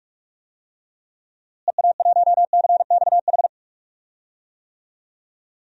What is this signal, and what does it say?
Morse code, heard as a keyed tone, about 47 wpm, 700 Hz. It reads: EA1CXH
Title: DXCC-100 Call Signs - 6 Characters - Repeated 3x - 45wpm